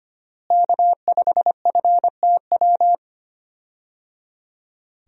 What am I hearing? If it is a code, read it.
Morse code, heard as a keyed tone, about 25 wpm, 700 Hz. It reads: K5FTW
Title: USA Call Signs - 5 Characters - Repeated 3x - 25wpm